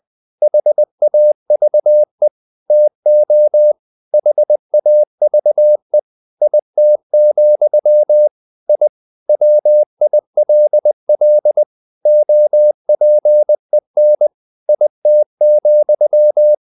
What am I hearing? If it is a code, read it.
Morse code, heard as a keyed tone, about 20 wpm, 600 Hz. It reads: HAVE TO HAVE IT, I WILL OPEN IT,
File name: the old man and the sea / the old man and the sea part 1 20wpm